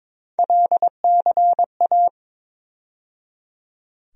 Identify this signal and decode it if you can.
Morse code, heard as a keyed tone, about 22 wpm, 700 Hz. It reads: LCA
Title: USA Call Sign Postfixes - 3 Letters 22wpm